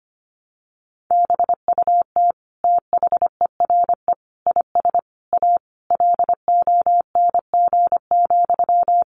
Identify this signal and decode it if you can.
Morse code, heard as a keyed tone, about 25 wpm, 700 Hz. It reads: BUT THERE IS A LONG,
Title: Sentences from Top 1100 Words 25wpm